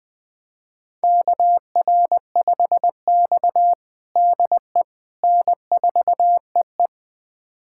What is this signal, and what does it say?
Morse code, heard as a keyed tone, about 20 wpm, 700 Hz. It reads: KR5X DE N4EE
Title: USA Callsign DE Callsign - 4 Characters - 20wpm